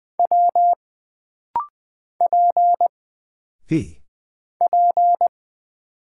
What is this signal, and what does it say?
Morse code, heard as a keyed tone, about 20 wpm, 700 Hz. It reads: W P P
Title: Single Character (Letter:Number:Punctuation:Prosign) - Rapid-Fire - 20wpm